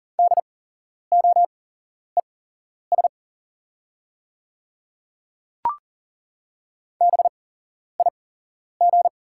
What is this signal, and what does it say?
Morse code, heard as a keyed tone, about 40 wpm, spaced as for 11 wpm, 690 Hz. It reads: DOES BIG
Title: 8x Character Spacing - Top 200 Words - 40wpm